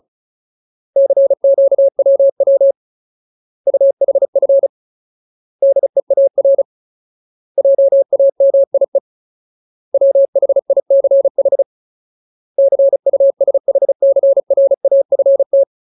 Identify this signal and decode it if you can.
Morse code, heard as a keyed tone, about 35 wpm, 550 Hz. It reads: CQWW UHF DEAR JAMIE WHICH CUSHCRAFT
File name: Ultimate Word List / 35WPM Ultimate